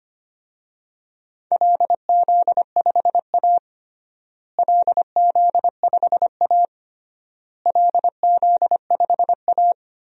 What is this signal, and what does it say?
Morse code, heard as a keyed tone, about 25 wpm, 700 Hz. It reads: LZ5A LZ5A LZ5A
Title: DXCC-100 Call Signs - 4 Characters - Repeated 3x - 25wpm